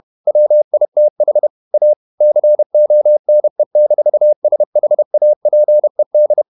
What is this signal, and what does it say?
Morse code, heard as a keyed tone, about 31 wpm, 600 Hz. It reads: WITH A CONE-SHAPED
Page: Never 31wpm | CW med Gnister